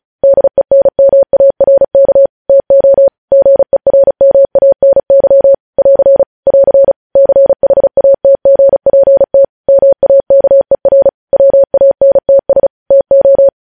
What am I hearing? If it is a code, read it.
Morse code, heard as a keyed tone, about 36 wpm, 550 Hz. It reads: DENMARK TO GERMANY <AR> <AR> CHATGPT MAKER WANTS TO
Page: News Headlines in Morse Code at 35 WPM